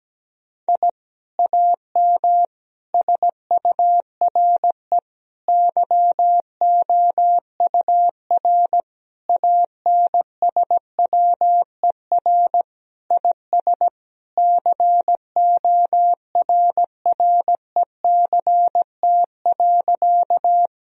Morse code, heard as a keyed tone, about 17 wpm, 700 Hz. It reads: I AM SURE YOUR ANSWER IS CORRECT.
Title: Sentences from Top 1000 Words - Review Part 2 - 17wpm